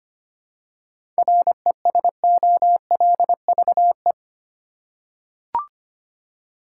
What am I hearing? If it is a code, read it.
Morse code, heard as a keyed tone, about 25 wpm, 700 Hz. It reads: RESOLVE
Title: Seven Letter Words - 25wpm